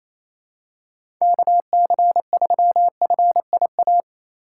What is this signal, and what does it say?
Morse code, heard as a keyed tone, about 28 wpm, 700 Hz. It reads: KC3FIA
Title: USA Call Signs - 6 Characters - 4x Character Spacing - 28wpm